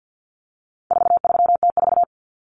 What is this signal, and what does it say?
Morse code, heard as a keyed tone, about 50 wpm, 700 Hz. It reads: 43T4
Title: Common Cut Numbers - 4-Digit 50wpm